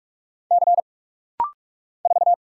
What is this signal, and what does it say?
Morse code, heard as a keyed tone, about 46 wpm, 700 Hz. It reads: C 4
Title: Single Letter-Number Alt - Rapid-Fire 45wpm